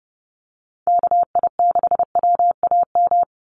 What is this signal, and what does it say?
Morse code, heard as a keyed tone, about 30 wpm, 700 Hz. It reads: KI6WAM
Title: USA Call Signs 30wpm